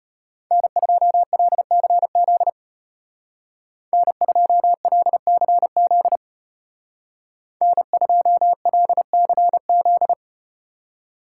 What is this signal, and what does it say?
Morse code, heard as a keyed tone, about 34 wpm, 700 Hz. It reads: N2LCZ N2LCZ N2LCZ